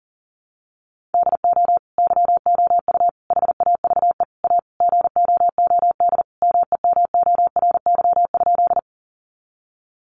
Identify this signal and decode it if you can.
Morse code, heard as a keyed tone, about 37 wpm, 700 Hz. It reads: DO YOU HAVE A GOOD MEMORY?
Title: Sentences from Top 1300 Words 40wpm